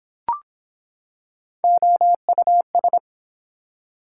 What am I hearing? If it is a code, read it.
Morse code, heard as a keyed tone, about 23 wpm, 700 Hz. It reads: OUS